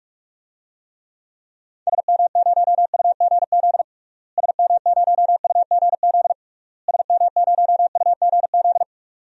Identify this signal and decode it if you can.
Morse code, heard as a keyed tone, about 41 wpm, 700 Hz. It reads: SM0UGZ SM0UGZ SM0UGZ
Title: DXCC-100 Call Signs - 6 Characters - Repeated 3x - 45wpm